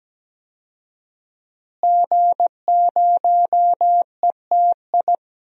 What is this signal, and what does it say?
Morse code, heard as a keyed tone, about 17 wpm, 700 Hz. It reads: G0ETI
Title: DXCC-100 Call Signs - 5 Characters - Repeated 3x - 17wpm